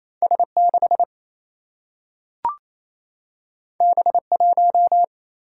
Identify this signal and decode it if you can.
Morse code, heard as a keyed tone, about 28 wpm, 700 Hz. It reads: S6 B1
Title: Single-Letter Single-Number 28wpm